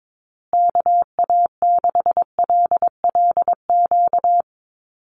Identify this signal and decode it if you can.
Morse code, heard as a keyed tone, about 22 wpm, 700 Hz. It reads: KA6LLQ
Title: USA Call Signs Repeated 3x 22wpm